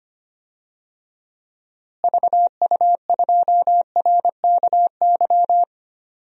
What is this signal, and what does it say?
Morse code, heard as a keyed tone, about 25 wpm, 700 Hz. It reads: VU2RKY
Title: DXCC-100 Call Signs - 6 Characters - Repeated 3x - 25wpm